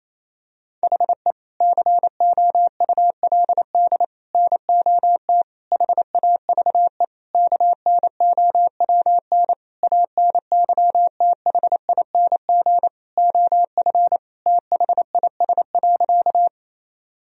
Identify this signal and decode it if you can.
Morse code, heard as a keyed tone, about 28 wpm, 700 Hz. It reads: HE COULD NOT HAVE KNOWN ANYTHING OF THIS.
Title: Sentences from Top 1000 Words - Review Part 3 - 28wpm